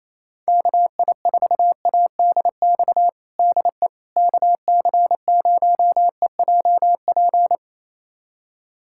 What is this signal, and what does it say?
Morse code, heard as a keyed tone, about 28 wpm, 700 Hz. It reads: KI4ADX DE KC0EJP
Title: USA Callsign DE Callsign - 6 Characters - 28wpm